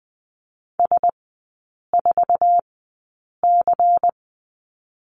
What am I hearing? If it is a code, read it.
Morse code, heard as a keyed tone, about 20 wpm, spaced as for 8 wpm, 700 Hz. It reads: S4C
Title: 107 Lesson 18e - Letter m - Three Characters 20wpm